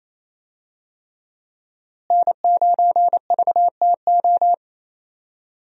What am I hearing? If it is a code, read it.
Morse code, heard as a keyed tone, about 28 wpm, 700 Hz. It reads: N9VTO
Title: USA Call Signs - 5 Characters - Repeated 3x - 28wpm